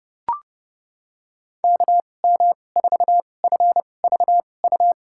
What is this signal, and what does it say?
Morse code, heard as a keyed tone, about 32 wpm, spaced as for 22 wpm, 700 Hz. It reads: KM4FVU
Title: USA Call Signs - 6 Characters - 2x Character Spacing - 30wpm